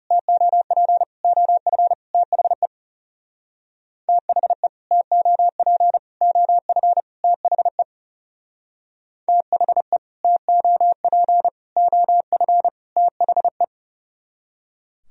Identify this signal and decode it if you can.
Morse code, heard as a keyed tone, about 35 wpm, 700 Hz. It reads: TOP OF THE THE TOP OF THE THE TOP OF THE
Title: Speed-Racing - Sets of 4 Words 30wpm